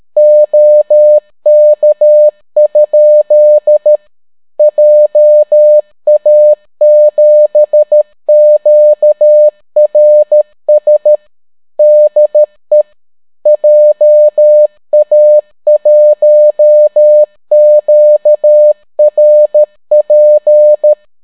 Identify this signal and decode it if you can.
Morse code, heard as a keyed tone, about 13 wpm, 600 Hz. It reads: OK? JA7QRS DE JA1QRP